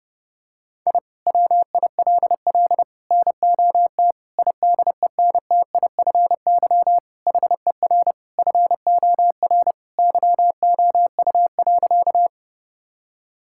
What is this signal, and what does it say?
Morse code, heard as a keyed tone, about 32 wpm, 700 Hz. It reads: I WILL NOT IDENTIFY HER FOR YOU.
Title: Sentences from Top 1000 Words - Review Part 6 - 30wpm